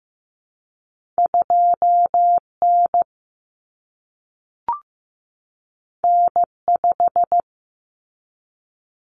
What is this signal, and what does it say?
Morse code, heard as a keyed tone, about 15 wpm, 700 Hz. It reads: 2 N N 5
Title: Common Cut Numbers - 2-Digit 15wpm